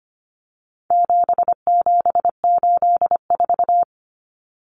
Morse code, heard as a keyed tone, about 25 wpm, 700 Hz. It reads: 7784
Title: Common Cut Numbers - 4-Digit 25wpm